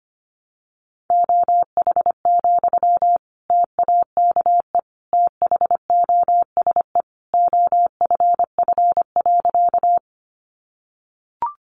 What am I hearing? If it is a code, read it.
Morse code, heard as a keyed tone, about 25 wpm, 700 Hz. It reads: OH, TAKE THOSE OFF.
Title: Sentences from Top 500 Words 25wpm